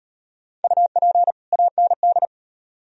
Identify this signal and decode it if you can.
Morse code, heard as a keyed tone, about 37 wpm, 700 Hz. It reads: UP AND